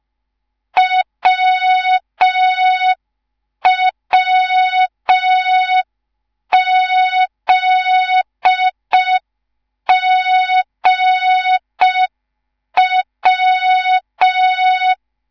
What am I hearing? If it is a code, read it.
Morse code, heard as a keyed tone, about 5 wpm, 3700 Hz. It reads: WWZGW
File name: Morze-HA2MN / hang / 10GWZ HA2MN